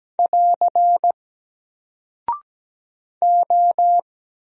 Morse code, heard as a keyed tone, about 17 wpm, 700 Hz. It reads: <AR> O
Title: Single Character (Letter:Number:Punctuation:Prosign) 17wpm